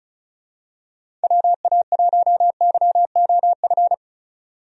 Morse code, heard as a keyed tone, about 36 wpm, 700 Hz. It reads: WA1YOF
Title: USA Call Signs - 6 Characters - 35wpm